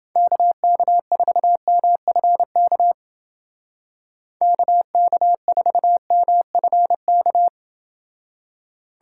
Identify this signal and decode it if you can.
Morse code, heard as a keyed tone, about 30 wpm, 700 Hz. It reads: KK4MFK KK4MFK